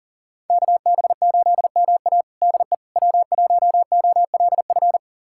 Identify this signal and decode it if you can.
Morse code, heard as a keyed tone, about 40 wpm, 700 Hz. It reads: KB8MA DE W1OLF
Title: USA Callsign DE Callsign - 5 Characters - 40wpm